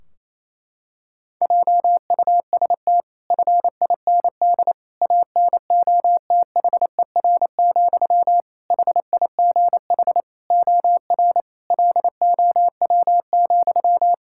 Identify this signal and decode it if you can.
Morse code, heard as a keyed tone, about 27 wpm, 700 Hz. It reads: JUST FIND ANOTHER, HIGH OR LOW,
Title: Sentences from Top 1600 Words 28wpm